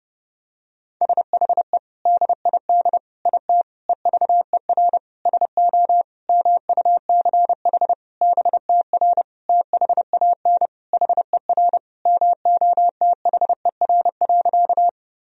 Morse code, heard as a keyed tone, about 29 wpm, 700 Hz. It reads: SHE DID IT EVER SO MUCH BTR THAN HER MOTHER.
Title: Sentences from Top 400 Words and Abbreviations 30wpm